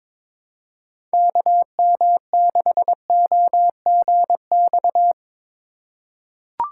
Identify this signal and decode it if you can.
Morse code, heard as a keyed tone, about 22 wpm, 700 Hz. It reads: KM6OGX
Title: USA Call Signs - 6 Characters - 4x Character Spacing - 22wpm